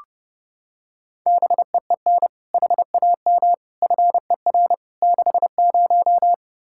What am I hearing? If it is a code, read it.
Morse code, heard as a keyed tone, about 30 wpm, 700 Hz. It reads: BEEN HAM FER 60
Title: Practice QSO - Exchange 3 - More Info - 30wpm